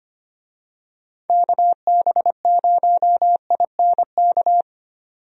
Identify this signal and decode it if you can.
Morse code, heard as a keyed tone, about 25 wpm, 700 Hz. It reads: KB0INK
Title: USA Call Signs - 6 Characters - 25wpm